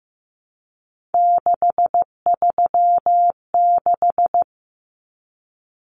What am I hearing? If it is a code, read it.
Morse code, heard as a keyed tone, about 15 wpm, 700 Hz. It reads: 636
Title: Numbers - 3-Digit 15wpm